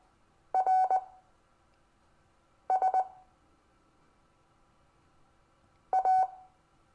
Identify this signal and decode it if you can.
Morse code, heard as a keyed tone, about 19 wpm, spaced as for 5 wpm, 750 Hz. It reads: RS A
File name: cw / Lesson26 SUMMARY